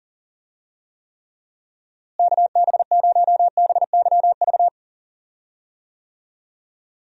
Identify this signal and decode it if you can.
Morse code, heard as a keyed tone, about 40 wpm, 700 Hz. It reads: KB0BYV
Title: USA Call Signs - 6 Characters - Repeated 3x - 40wpm